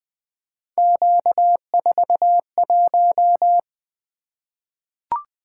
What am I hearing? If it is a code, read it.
Morse code, heard as a keyed tone, about 20 wpm, 700 Hz. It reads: Q41
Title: Three Letter-Number Combinations - 20wpm